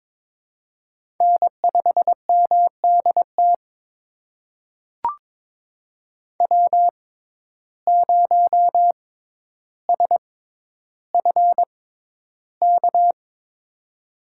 Morse code, heard as a keyed tone, about 22 wpm, spaced as for 11 wpm, 700 Hz. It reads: N5MDT W0SFK
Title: USA Call Signs - 6x Character Spacing - 22wpm